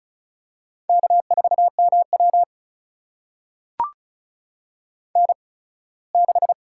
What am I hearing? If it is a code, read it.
Morse code, heard as a keyed tone, about 36 wpm, spaced as for 21 wpm, 700 Hz. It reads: K4MW N6
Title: USA Call Signs - 8x Character Spacing - 35wpm